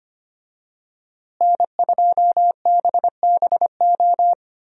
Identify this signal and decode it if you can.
Morse code, heard as a keyed tone, about 25 wpm, 700 Hz. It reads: N2BBO
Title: USA Call Signs - 5 Characters - Repeated 2x - 25wpm